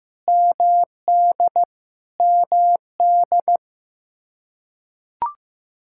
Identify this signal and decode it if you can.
Morse code, heard as a keyed tone, about 15 wpm, 700 Hz. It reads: MD MD
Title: Practice QSO - Exchange 1 - Beginning - 15wpm